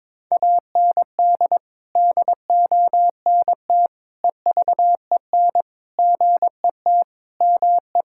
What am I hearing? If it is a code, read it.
Morse code, heard as a keyed tone, about 22 wpm, 700 Hz. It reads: AND DONT EVEN GET ME
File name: practice / 22wpm / Sets of 8 Words - 22wpm